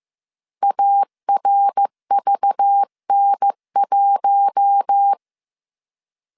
Morse code, heard as a keyed tone, about 15 wpm, 800 Hz. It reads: ARVN1
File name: CW / DET / 1 6 1500C S B 14wpm 2958